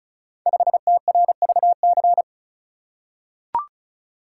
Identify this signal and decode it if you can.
Morse code, heard as a keyed tone, about 36 wpm, 700 Hz. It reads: 5TRVC
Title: Code Groups - 5 Letter:Numbers - 35wpm